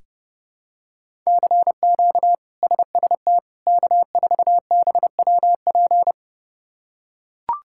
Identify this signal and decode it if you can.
Morse code, heard as a keyed tone, about 30 wpm, 700 Hz. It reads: CQ SST K4BWP
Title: K1USN SST QSO Practice - 30wpm